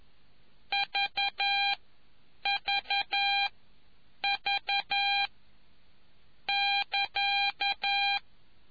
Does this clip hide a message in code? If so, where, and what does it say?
Yes: Morse code from 0.7 to 8.2 s, heard as a keyed tone, about 11 wpm, spaced as for 7 wpm, 2400 Hz. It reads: VVV<KA>